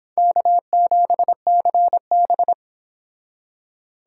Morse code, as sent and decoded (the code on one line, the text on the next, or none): -.- --... -.-. -...
K7CB